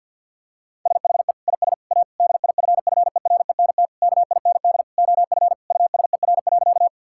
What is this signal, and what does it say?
Morse code, heard as a keyed tone, about 50 wpm, 700 Hz. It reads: SHE IS A DIFFERENT KIND OF USER.
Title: Sentences from Top 1000 Words - Review Part 5 - 50wpm